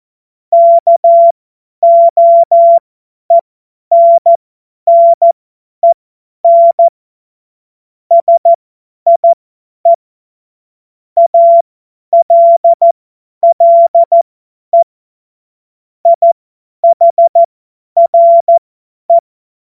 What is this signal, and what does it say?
Morse code, heard as a keyed tone, about 14 wpm, spaced as for 10 wpm, 680 Hz. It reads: KOENNEN SIE ALLE IHRE
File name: Witz-23-September-2025-50-F70